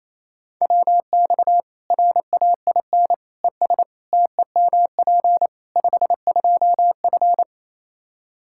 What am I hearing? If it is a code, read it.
Morse code, heard as a keyed tone, about 28 wpm, 700 Hz. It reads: WX RAIN ES TEMP 52F
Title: Practice QSO - Exchange 2 - Info - 28wpm